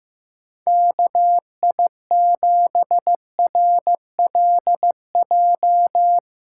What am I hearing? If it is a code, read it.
Morse code, heard as a keyed tone, about 16 wpm, 700 Hz. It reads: KI7RLJ
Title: Speed Racing - USA 2x3 Call Signs 15wpm